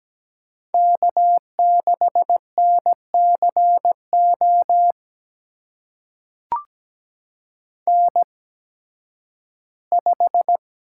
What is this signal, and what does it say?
Morse code, heard as a keyed tone, about 17 wpm, spaced as for 11 wpm, 700 Hz. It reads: K6NCO N5
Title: USA Call Signs - 5 Characters - 8x Character Spacing - 17wpm